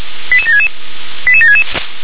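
Five-tone selective call (ZVEI), heard as quick runs of five tones, tones of about 70 ms. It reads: 88766 88766
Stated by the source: ZVEI five-tone alert received by SDR.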